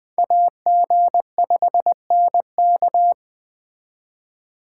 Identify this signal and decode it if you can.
Morse code, heard as a keyed tone, about 20 wpm, 700 Hz. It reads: AG5NK
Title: Speed Racing - USA 2x2 Call Signs 17wpm